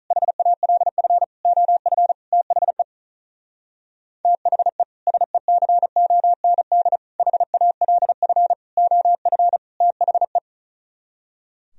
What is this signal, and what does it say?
Morse code, heard as a keyed tone, about 46 wpm, 700 Hz. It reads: HALF OF THE THE SECOND HALF OF THE